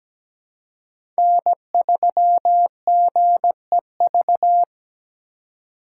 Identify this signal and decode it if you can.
Morse code, heard as a keyed tone, about 17 wpm, 700 Hz. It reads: N3GEV
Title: Code Groups - 5 Letter:Numbers - 17wpm